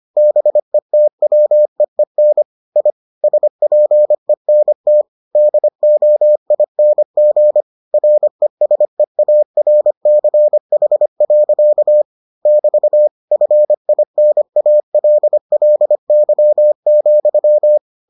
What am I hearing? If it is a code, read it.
Morse code, heard as a keyed tone, about 25 wpm, 600 Hz. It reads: BETWEEN I SPENT DOING RESEARCH. <BT> FINALLY,